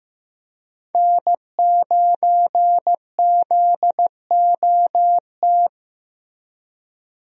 Morse code, heard as a keyed tone, about 15 wpm, 700 Hz. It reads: N9ZOT
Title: USA Call Signs - 5 Characters - Repeated 2x - 15wpm